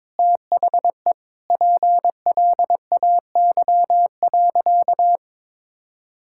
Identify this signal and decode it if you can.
Morse code, heard as a keyed tone, about 22 wpm, 700 Hz. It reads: THE PLAY.
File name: practice / 22wpm / Sentences with 4-Word Phrases - 22wpm